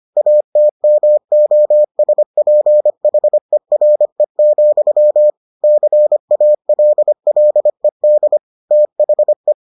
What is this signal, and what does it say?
Morse code, heard as a keyed tone, about 25 wpm, 600 Hz. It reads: ATMOSPHERE, CALLED THE